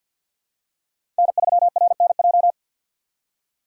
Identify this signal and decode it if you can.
Morse code, heard as a keyed tone, about 51 wpm, 700 Hz. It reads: N3RNJ
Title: USA Call Signs - 5 Characters - 3x Character Spacing - 50wpm